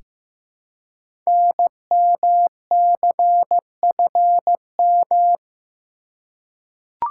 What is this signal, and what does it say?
Morse code, heard as a keyed tone, about 15 wpm, 700 Hz. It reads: NMCFM
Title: Code Groups - 5 Letters - 15wpm